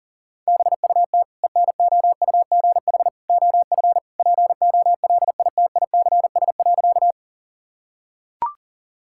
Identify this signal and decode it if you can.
Morse code, heard as a keyed tone, about 40 wpm, 700 Hz. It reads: BUT ENOUGH OF POLITICS.
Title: Sentences from Top 1000 Words - Review Part 6 - 40wpm